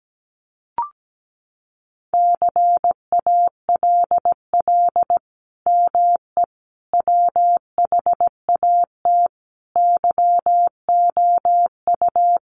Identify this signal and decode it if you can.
Morse code, heard as a keyed tone, about 17 wpm, 700 Hz. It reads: CALL ME WHAT YOU
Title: Sentences from Top 200 Words 17wpm